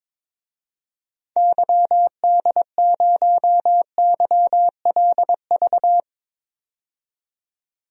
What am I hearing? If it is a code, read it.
Morse code, heard as a keyed tone, about 23 wpm, 700 Hz. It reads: YD0YLV
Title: DXCC-100 Call Signs - 6 Characters - 22wpm